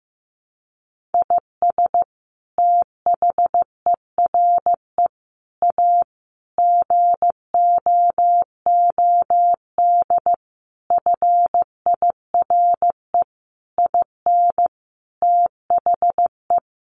Morse code, heard as a keyed tone, about 15 wpm, 700 Hz. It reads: IS THERE A GOOD FIRE IN THE
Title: Sentences from Top 1400 Words 15wpm